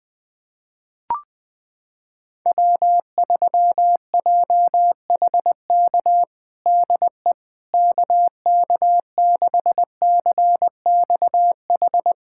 Morse code, heard as a keyed tone, about 20 wpm, 700 Hz. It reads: W3JHK DE KK6CXH
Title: USA Callsign DE Callsign 20wpm